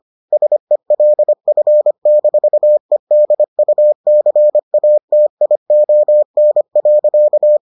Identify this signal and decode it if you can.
Morse code, heard as a keyed tone, about 25 wpm, 600 Hz. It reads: SELF-EDUCATION.